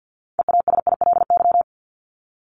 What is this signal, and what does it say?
Morse code, heard as a keyed tone, about 50 wpm, 700 Hz. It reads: EASILY
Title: LY Suffix - Top 100 Words with Base Words 50wpm